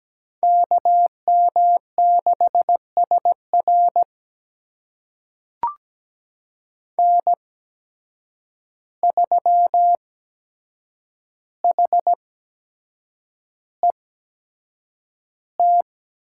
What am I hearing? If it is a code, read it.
Morse code, heard as a keyed tone, about 17 wpm, spaced as for 7 wpm, 700 Hz. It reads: KM6SR N3HET
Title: USA Call Signs - 5 Characters - 8x Character Spacing - 17wpm